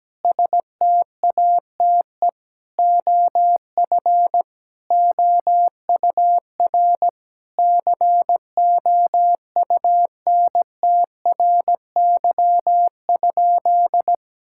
Morse code, heard as a keyed tone, about 17 wpm, 700 Hz. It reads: STATE OF OUR COUNTRY?
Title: Sentences from Top 1000 Words - Review Part 5 - 17wpm